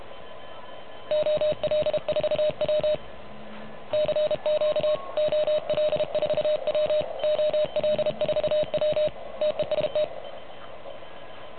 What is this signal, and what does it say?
Morse code, heard as a keyed tone, about 31 wpm, 610 Hz. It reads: OL4W CQ OL4W OL4W TEST
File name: truSDX / MWC29.8